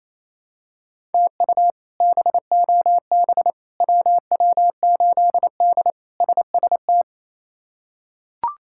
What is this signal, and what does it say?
Morse code, heard as a keyed tone, about 28 wpm, 700 Hz. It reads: TU BOB WW8D SST
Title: K1USN SST QSO Practice - 28wpm